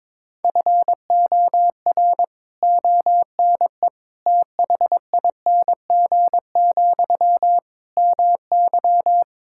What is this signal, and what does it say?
Morse code, heard as a keyed tone, about 22 wpm, 700 Hz. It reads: FOR ONE THING, MY
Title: Sentences from Top 1000 Words - Review Part 6 - 22wpm